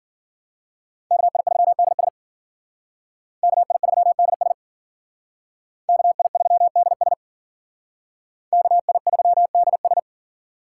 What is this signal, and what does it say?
Morse code, heard as a keyed tone, about 51 wpm, 700 Hz. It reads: KI3DS KI3DS KI3DS KI3DS